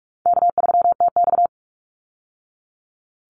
Morse code, heard as a keyed tone, about 44 wpm, 700 Hz. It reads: K3TX
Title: Easy Call Signs Repeated 2x - 45wpm